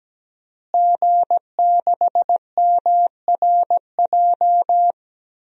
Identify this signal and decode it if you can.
Morse code, heard as a keyed tone, about 17 wpm, 700 Hz. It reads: G6MRJ
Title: DXCC-100 Call Signs - 5 Characters - Repeated 3x - 17wpm